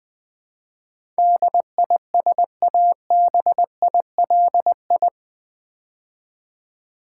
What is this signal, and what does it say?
Morse code, heard as a keyed tone, about 20 wpm, 700 Hz. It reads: DISABILI
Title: Build A Word - 4 Letters At A Time - 10-Letter Words - 20wpm